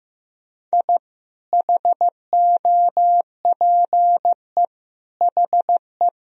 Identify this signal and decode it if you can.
Morse code, heard as a keyed tone, about 16 wpm, 700 Hz. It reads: I HOPE HE
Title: Sentences from Top 1000 Words - Review Part 3 - 15wpm